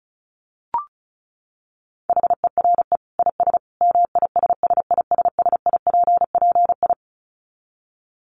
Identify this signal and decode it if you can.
Morse code, heard as a keyed tone, about 34 wpm, 700 Hz. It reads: HERE IS MISSISSIPPI
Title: Dit Challenge 35wpm